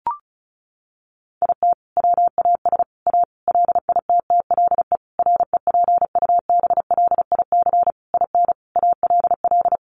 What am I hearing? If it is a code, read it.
Morse code, heard as a keyed tone, about 34 wpm, 700 Hz. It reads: IT WAS A LITTLE REPUBLIC IN ALL
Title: Sentences from Top 1500 Words 35wpm